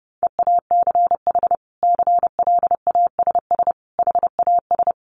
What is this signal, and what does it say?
Morse code, heard as a keyed tone, about 30 wpm, 700 Hz. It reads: EACH CLASS HAS
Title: Sentences from Top 1400 Words 30wpm